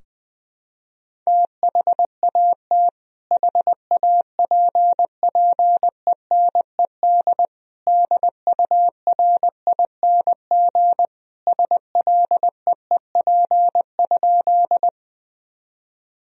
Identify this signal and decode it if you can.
Morse code, heard as a keyed tone, about 20 wpm, 700 Hz. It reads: THAT HAPPENED DURING SLEEP?
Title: Sentences from Top 1000 Words - Review Part 2 - 20wpm